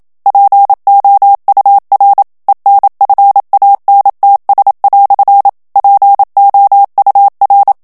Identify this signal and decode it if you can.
Morse code, heard as a keyed tone, about 27 wpm, 800 Hz. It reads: POUR ENFANTS" POUR
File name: FRANCAIS TEXTE CLAIR 1 VITESSE 26 MOTS MINUTES